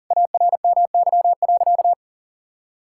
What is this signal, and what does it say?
Morse code, heard as a keyed tone, about 40 wpm, 700 Hz. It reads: ARMY.
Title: Sentences from Top 1000 Words - Review Part 5 - 40wpm